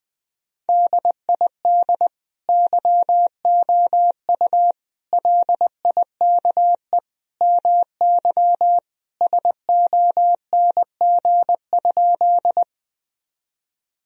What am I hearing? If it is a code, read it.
Morse code, heard as a keyed tone, about 20 wpm, 700 Hz. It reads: DID YOU LIKE MY SONG?
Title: Sentences from Top 1000 Words - Review Part 2 - 20wpm